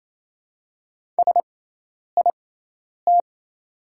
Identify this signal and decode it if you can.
Morse code, heard as a keyed tone, about 27 wpm, spaced as for 10 wpm, 690 Hz. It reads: SIT
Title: 6x Character Spacing - Top 500 Words - 28wpm